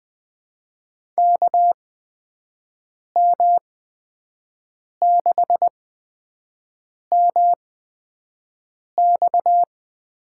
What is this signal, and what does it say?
Morse code, heard as a keyed tone, about 20 wpm, spaced as for 5 wpm, 700 Hz. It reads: KM6MX
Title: USA Call Signs - 6 Characters - 8x Character Spacing - 20wpm